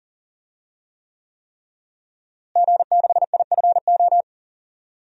Morse code, heard as a keyed tone, about 40 wpm, 700 Hz. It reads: G6IFO
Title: DXCC-100 Call Signs - 5 Characters - Repeated 3x - 40wpm